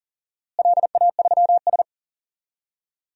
Morse code, heard as a keyed tone, about 39 wpm, 700 Hz. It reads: LA3S
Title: DXCC-100 Call Signs - 4 Characters - Repeated 2x - 40wpm